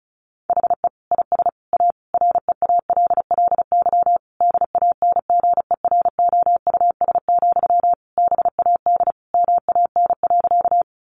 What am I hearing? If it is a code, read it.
Morse code, heard as a keyed tone, about 34 wpm, 700 Hz. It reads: HE IS A REALLY DANGEROUS, BAD MAN.
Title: Sentences from Top 1200 Words 35wpm